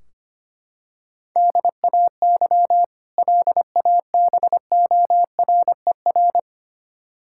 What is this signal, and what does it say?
Morse code, heard as a keyed tone, about 25 wpm, 700 Hz. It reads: DAY LABORER
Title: K1USN SST QSO Practice 25wpm